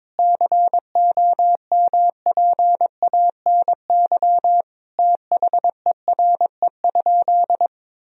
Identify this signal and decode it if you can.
Morse code, heard as a keyed tone, about 21 wpm, 700 Hz. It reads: COMPANY THERE?
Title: Sentences from Top 1000 Words - Review Part 3 - 22wpm